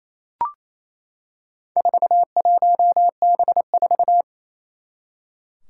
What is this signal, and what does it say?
Morse code, heard as a keyed tone, about 29 wpm, 700 Hz. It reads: 41B4
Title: Four Challenging Letter-Number Combinations 28wpm